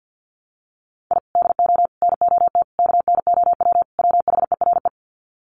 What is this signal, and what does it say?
Morse code, heard as a keyed tone, about 50 wpm, 700 Hz. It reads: I DO NOT KNOW WHERE
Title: Sets of 5 Words 50wpm